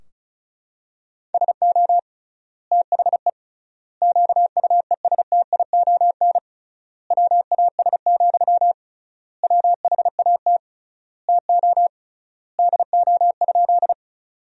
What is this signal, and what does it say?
Morse code, heard as a keyed tone, about 36 wpm, 700 Hz. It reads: SO THE QUESTION WAS, WHAT TO DO?
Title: 3x Word-Spacing - Sentences from Top 400 Words - 35wpm